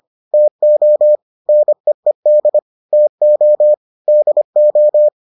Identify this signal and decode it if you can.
Morse code, heard as a keyed tone, about 25 wpm, 600 Hz. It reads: TO NEED TO DO